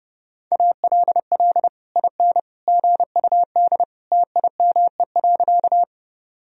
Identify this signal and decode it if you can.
Morse code, heard as a keyed tone, about 30 wpm, 700 Hz. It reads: ALL IN GUD TIME.
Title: Sentences from Top 200 Words and Abbreviations 30wpm